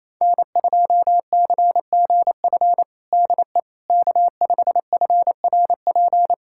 Morse code, heard as a keyed tone, about 27 wpm, 700 Hz. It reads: N2CGF DE K5FRP
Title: USA Callsign DE Callsign - 5 Characters - 28wpm